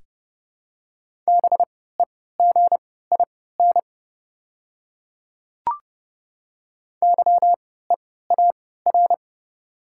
Morse code, heard as a keyed tone, about 30 wpm, spaced as for 17 wpm, 690 Hz. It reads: BEGIN YEAR
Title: 3x Character Spacing - Top 300 Words - 30wpm